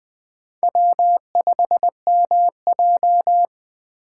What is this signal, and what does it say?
Morse code, heard as a keyed tone, about 20 wpm, 700 Hz. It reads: W5MJ
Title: USA Call Signs - 4 Characters - 6x Character Spacing - 20wpm